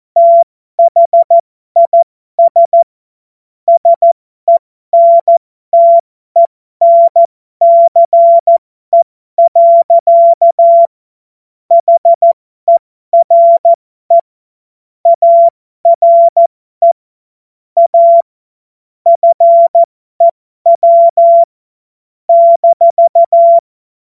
Morse code, heard as a keyed tone, about 14 wpm, spaced as for 12 wpm, 680 Hz. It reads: THIS SENTENCE. HERE ARE A FEW -
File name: Morsekurs / Witze